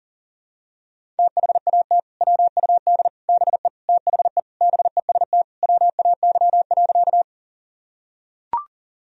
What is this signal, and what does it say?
Morse code, heard as a keyed tone, about 40 wpm, 700 Hz. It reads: THAT WUD BE THE BEST WAY.
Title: Sentences from Top 400 Words and Abbreviations 40wpm